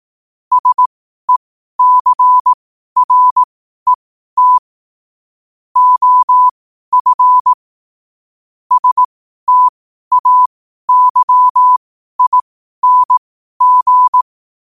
Morse code, heard as a keyed tone, about 18 wpm, spaced as for 12 wpm, 1000 Hz. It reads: SECRET OF STAYING
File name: qotdfiles / 12wpm